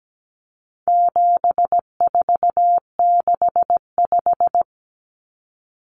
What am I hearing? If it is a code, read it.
Morse code, heard as a keyed tone, about 17 wpm, 700 Hz. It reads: 7465
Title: Numbers - 4-Digit 17wpm